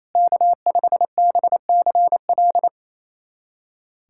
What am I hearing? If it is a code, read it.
Morse code, heard as a keyed tone, about 28 wpm, 700 Hz. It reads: K5BCL
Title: USA Callsign DE Callsign - 5 Characters - 28wpm